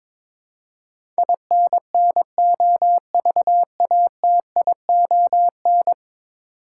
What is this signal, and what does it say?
Morse code, heard as a keyed tone, about 22 wpm, 700 Hz. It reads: INNOVATION
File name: practice / 22wpm / 10 Letter Words - First 3 Letters Given - 22wpm